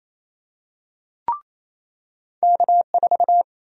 Morse code, heard as a keyed tone, about 28 wpm, 700 Hz. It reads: K4
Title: USA Call Sign Prefixes - 28wpm